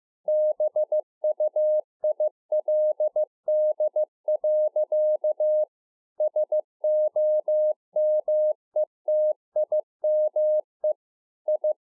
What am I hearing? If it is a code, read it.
Morse code, heard as a keyed tone, about 15 wpm, 610 Hz. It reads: BUILD. SOMETIME I